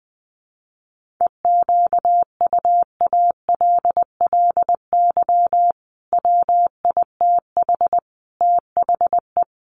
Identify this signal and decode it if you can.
Morse code, heard as a keyed tone, about 20 wpm, 700 Hz. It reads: EQUALLY WITH THE
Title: Sentences from Top 1400 Words 20wpm